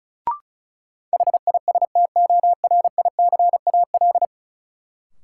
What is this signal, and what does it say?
Morse code, heard as a keyed tone, about 36 wpm, 700 Hz. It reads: HISTORICAL
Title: Sentences from Top 600 Words - New Word Drill - 35wpm